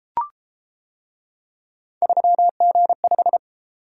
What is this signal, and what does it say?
Morse code, heard as a keyed tone, about 31 wpm, 700 Hz. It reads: 3G5